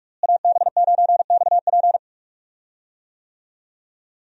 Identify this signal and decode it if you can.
Morse code, heard as a keyed tone, about 42 wpm, 700 Hz. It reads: AB9XP
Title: USA Call Signs - 5 Characters - Repeated 2x - 45wpm